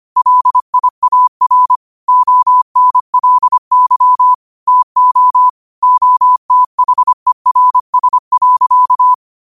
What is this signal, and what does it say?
Morse code, heard as a keyed tone, about 25 wpm, 1000 Hz. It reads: LIAR ONLY TO OTHERS.